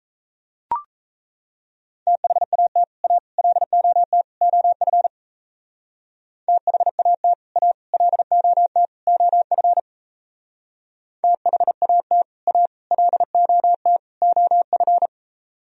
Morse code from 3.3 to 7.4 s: LOT OF THAT

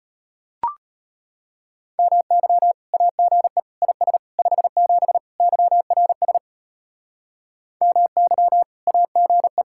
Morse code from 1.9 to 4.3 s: MY AGE IS

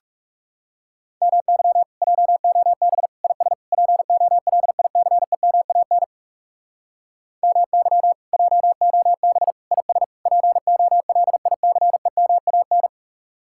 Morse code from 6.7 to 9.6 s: MY JOB

